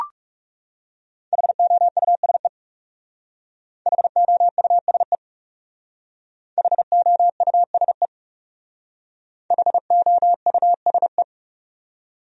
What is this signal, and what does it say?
Morse code, heard as a keyed tone, about 40 wpm, 700 Hz. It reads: HOUSE HOUSE HOUSE HOUSE